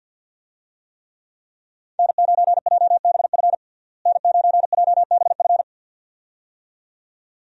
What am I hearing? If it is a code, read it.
Morse code, heard as a keyed tone, about 51 wpm, 700 Hz. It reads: N9JBF N9JBF
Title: USA Call Signs - 5 Characters - Repeated 2x - 50wpm